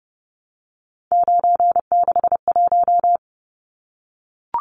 Morse code, heard as a keyed tone, about 30 wpm, 700 Hz. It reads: 961
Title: Numbers - 3-Digit 30wpm